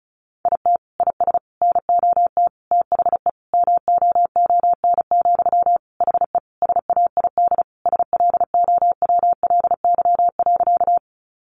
Morse code, heard as a keyed tone, about 34 wpm, 700 Hz. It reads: IT IS NOT THE MOON, HE SAID SLOWLY.
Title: Sentences from Top 1100 Words 35wpm